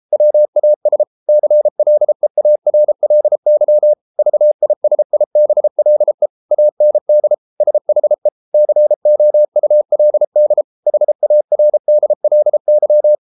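Morse code, heard as a keyed tone, about 33 wpm, 600 Hz. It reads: WAS CLEARLY VISIBLE AND SHE COULD HARDLY
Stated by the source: Morse code practice at 32 wpm from CW med Gnister, reading Ken Follett's novel Never in English.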